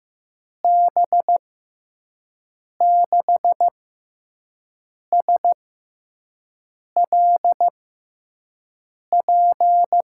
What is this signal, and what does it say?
Morse code, heard as a keyed tone, about 15 wpm, spaced as for 5 wpm, 700 Hz. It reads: B6SLP
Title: USA Call Signs - 6 Characters - 6x Character Spacing - 15wpm